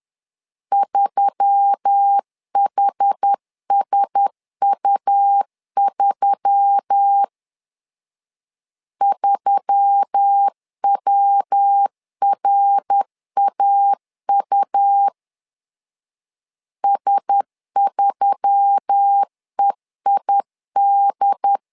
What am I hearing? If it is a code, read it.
Morse code, heard as a keyed tone, about 10 wpm, 800 Hz. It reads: 3HSU3 3WRAU S3EID
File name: LEZIONI / 4 Q1S3 10wpm 2427